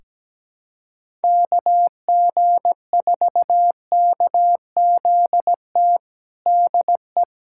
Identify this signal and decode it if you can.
Morse code, heard as a keyed tone, about 17 wpm, 700 Hz. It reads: KG4KZT DE
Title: USA Callsign DE Callsign 17wpm